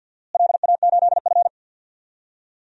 Morse code, heard as a keyed tone, about 51 wpm, 700 Hz. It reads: LA8F
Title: DXCC-100 Call Signs - 4 Characters - Repeated 3x - 50wpm